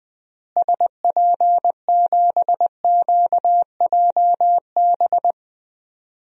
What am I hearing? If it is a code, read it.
Morse code, heard as a keyed tone, about 20 wpm, 700 Hz. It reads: SP7QJB
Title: DXCC-100 Call Signs - 6 Characters - Repeated 3x - 20wpm